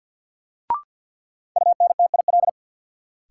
Morse code, heard as a keyed tone, about 53 wpm, 700 Hz. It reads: UNTIL
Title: Sentences from Top 500 Words - New Word Drill - 50wpm